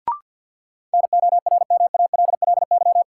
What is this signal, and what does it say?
Morse code, heard as a keyed tone, about 50 wpm, 700 Hz. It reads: NORMALLY
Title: Sentences from Top 1000 Words - Review Part 6 - New Word Drill - 50wpm